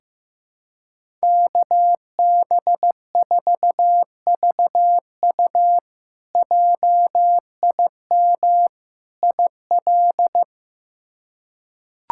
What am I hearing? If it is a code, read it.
Morse code, heard as a keyed tone, about 15 wpm, 700 Hz. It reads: KB4VU JIM IL
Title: K1USN SST QSO Practice - 15wpm